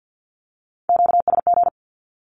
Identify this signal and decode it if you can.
Morse code, heard as a keyed tone, about 50 wpm, 700 Hz. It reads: QSG
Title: Q-Codes 50wpm